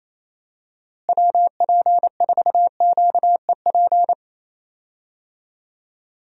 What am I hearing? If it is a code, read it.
Morse code, heard as a keyed tone, about 28 wpm, 700 Hz. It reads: WP4QEP
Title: USA Call Signs - 6 Characters - Repeated 3x - 28wpm